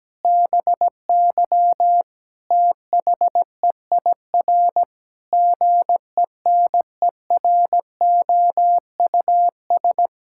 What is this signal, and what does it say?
Morse code, heard as a keyed tone, about 17 wpm, 700 Hz. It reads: BY THEIR GENEROUS
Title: Pangram Challenge v2 - 17wpm